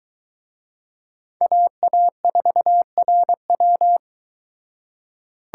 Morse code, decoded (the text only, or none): AA4RW